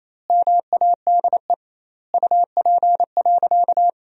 Morse code, heard as a keyed tone, about 28 wpm, 700 Hz. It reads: MADE UP.
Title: 2x Word-Spacing - Sentences from Top 500 Words - 28wpm